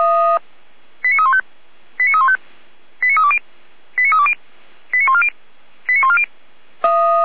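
Five-tone selective call (ZVEI), heard as quick runs of five tones, tones of about 70 ms. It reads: 89315 89315 89320 89320 89140 89140
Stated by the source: ZVEI five-tone call 89249 received by SDR, from feuerwehr-orlamuende.org fire brigade server.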